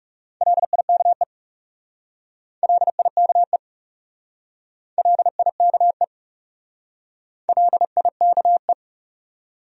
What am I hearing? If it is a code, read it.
Morse code, heard as a keyed tone, about 37 wpm, 700 Hz. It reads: LIKE LIKE LIKE LIKE
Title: Speed-Racing Top 200 Words 30wpm